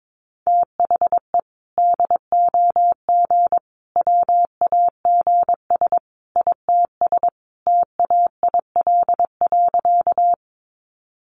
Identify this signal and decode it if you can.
Morse code, heard as a keyed tone, about 22 wpm, 700 Hz. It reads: THE DOG WAGS ITS TAIL.
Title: Easy Phrases 22wpm